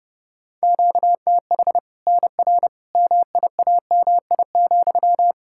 Morse code, heard as a keyed tone, about 32 wpm, 700 Hz. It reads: QTH NR MIAMI,
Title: Practice QSO - Exchange 1 - Beginning - 30wpm